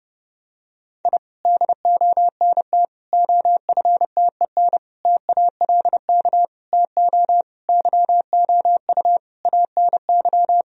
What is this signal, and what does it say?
Morse code, heard as a keyed tone, about 30 wpm, 700 Hz. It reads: I DONT OFTEN TALK TO YOU ANY
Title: Sentences from Top 1000 Words - Review Part 1 - 30wpm